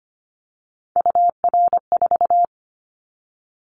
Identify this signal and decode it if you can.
Morse code, heard as a keyed tone, about 25 wpm, 700 Hz. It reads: UR4
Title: Callsign Prefixes for Top 25 DXCC - 25wpm